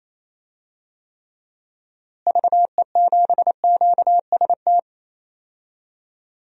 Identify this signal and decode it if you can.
Morse code, heard as a keyed tone, about 28 wpm, 700 Hz. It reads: VE7QST
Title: DXCC-100 Call Signs - 6 Characters - Repeated 3x - 28wpm